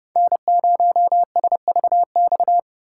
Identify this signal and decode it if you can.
Morse code, heard as a keyed tone, about 30 wpm, 700 Hz. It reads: N0SVX